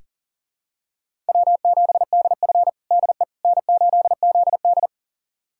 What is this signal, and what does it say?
Morse code, heard as a keyed tone, about 37 wpm, 700 Hz. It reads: W7DF DE N8ZD
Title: USA Callsign DE Callsign - 4 Characters - 40wpm